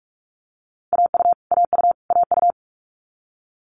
Morse code, heard as a keyed tone, about 46 wpm, 700 Hz. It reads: AU AU AU
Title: Multi-letter Phonemes Repeated 3x 45wpm